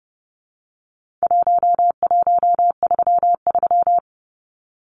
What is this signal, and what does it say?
Morse code, heard as a keyed tone, about 30 wpm, 700 Hz. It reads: 1133
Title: Numbers - 4-Digit 30wpm